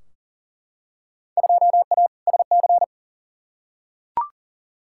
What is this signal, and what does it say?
Morse code, heard as a keyed tone, about 40 wpm, 700 Hz. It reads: 2A SC
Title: Field Day Exchanges - 40wpm